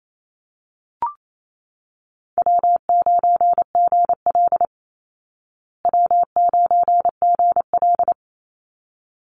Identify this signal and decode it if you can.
Morse code, heard as a keyed tone, about 28 wpm, 700 Hz. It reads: W9GL W9GL
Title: Call Signs Repeated 3x - 28wpm